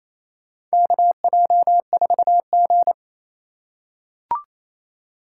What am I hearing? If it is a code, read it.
Morse code, heard as a keyed tone, about 28 wpm, 700 Hz. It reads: KJ4G
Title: USA Call Signs - 4 Characters - 3x Character Spacing - 28wpm